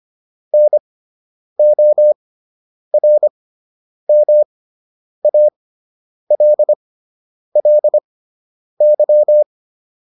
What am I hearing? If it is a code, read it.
Morse code, heard as a keyed tone, about 25 wpm, spaced as for 9 wpm, 600 Hz. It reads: NORMALLY